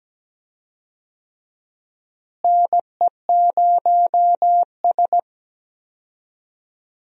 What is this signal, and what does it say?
Morse code, heard as a keyed tone, about 17 wpm, 700 Hz. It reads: NE0S
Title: USA Call Signs - 4 Characters - Repeated 3x - 17wpm